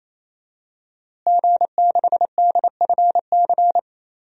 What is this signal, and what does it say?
Morse code, heard as a keyed tone, about 28 wpm, 700 Hz. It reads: G6DFC
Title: DXCC-100 Call Signs - 5 Characters - Repeated 3x - 28wpm